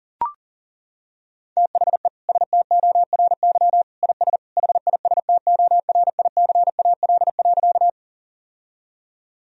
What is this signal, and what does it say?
Morse code, heard as a keyed tone, about 37 wpm, 700 Hz. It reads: THE STORY IS HISTORICAL.
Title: Sentences from Top 1000 Words - Review Part 4 - 40wpm